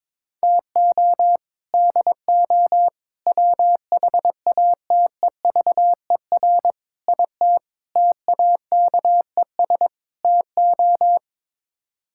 Morse code, heard as a keyed tone, about 22 wpm, 700 Hz. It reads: TO DO WHATEVER IT TAKES TO
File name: practice / 22wpm / Sets of 6 Words - 22wpm